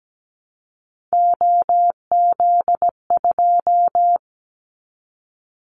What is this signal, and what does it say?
Morse code, heard as a keyed tone, about 17 wpm, 700 Hz. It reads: OZ2
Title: Callsign Prefixes for Top 50 DXCC - 17wpm